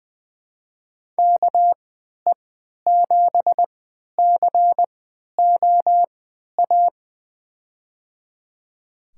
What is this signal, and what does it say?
Morse code, heard as a keyed tone, about 20 wpm, spaced as for 11 wpm, 700 Hz. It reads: KE7COA
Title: USA Call Signs - 6 Characters - 3x Character Spacing - 20wpm